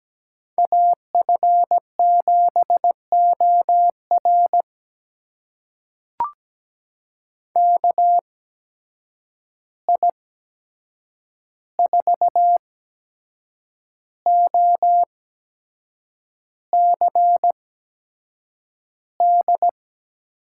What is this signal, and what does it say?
Morse code, heard as a keyed tone, about 17 wpm, spaced as for 7 wpm, 700 Hz. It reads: AF7OR KI4OCD
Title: USA Call Signs - 8x Character Spacing - 17wpm